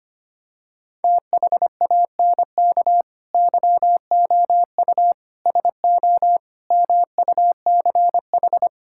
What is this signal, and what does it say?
Morse code, heard as a keyed tone, about 25 wpm, 700 Hz. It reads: THANK YOU SO MUCH
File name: practice / 25wpm / Sentences with 9-Word Phrases - 25wpm